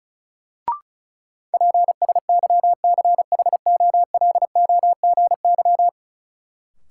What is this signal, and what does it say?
Morse code, heard as a keyed tone, about 34 wpm, 700 Hz. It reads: PSYCHOLOGY
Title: Sentences from Top 1400 Words - New Word Drill Alt - 35wpm